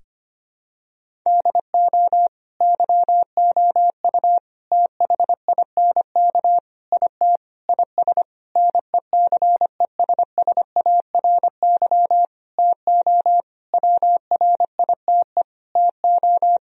Morse code, heard as a keyed tone, about 25 wpm, 700 Hz. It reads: DO YOU THINK IT IS NECESSARY TO WRITE TO
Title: Sentences from Top 1000 Words - Review Part 1 - 25wpm